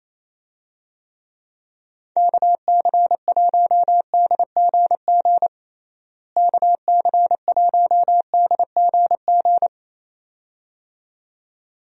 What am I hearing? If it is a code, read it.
Morse code, heard as a keyed tone, about 28 wpm, 700 Hz. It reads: KC1DGG KC1DGG
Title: USA Call Signs - 6 Characters - Repeated 2x - 28wpm